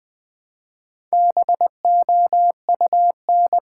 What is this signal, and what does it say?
Morse code, heard as a keyed tone, about 20 wpm, 700 Hz. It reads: BOUN
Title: Build A Word - 2 Letters At A Time - 8-Letter Words - 20wpm